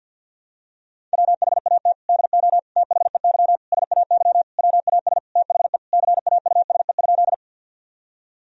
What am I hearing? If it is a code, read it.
Morse code, heard as a keyed tone, about 49 wpm, 700 Hz. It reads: WHAT DO THEY SAY WAS THE CAUSE?
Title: Sentences from Top 1000 Words - Review Part 1 - 50wpm